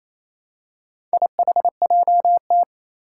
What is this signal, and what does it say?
Morse code, heard as a keyed tone, about 27 wpm, 700 Hz. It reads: IHJT
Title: Four Challenging Letter-Number Combinations 28wpm